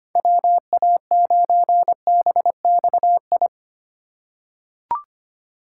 Morse code, heard as a keyed tone, about 25 wpm, 700 Hz. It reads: WA9BXI
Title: USA Call Signs - 2x Character Spacing - 25wpm